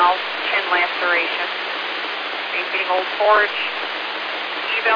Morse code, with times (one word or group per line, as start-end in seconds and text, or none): none